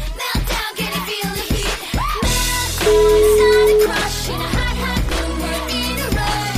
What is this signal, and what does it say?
Ringback tone (Europe, 425 Hz), with noise.